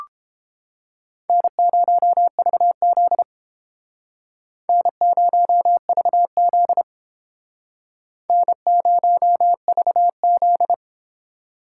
Morse code, heard as a keyed tone, about 30 wpm, 700 Hz. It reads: N0VZ N0VZ N0VZ